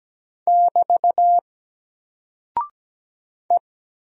Morse code, heard as a keyed tone, about 17 wpm, 700 Hz. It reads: <BT> E
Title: Single Character (Letter:Number:Punctuation:Prosign) Alt 17wpm